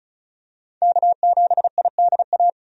Morse code, heard as a keyed tone, about 36 wpm, 700 Hz. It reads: K7IDA
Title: USA Call Signs - 5 Characters - 35wpm